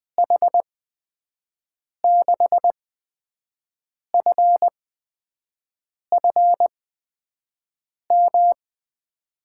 Morse code, heard as a keyed tone, about 21 wpm, spaced as for 5 wpm, 700 Hz. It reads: H6FFM
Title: USA Call Signs - 6 Characters - 8x Character Spacing - 20wpm